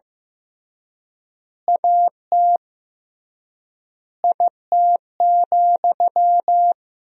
Morse code, heard as a keyed tone, about 16 wpm, 700 Hz. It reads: AT IT,
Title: 3x Word-Spacing - Sentences from Top 200 Words - 15wpm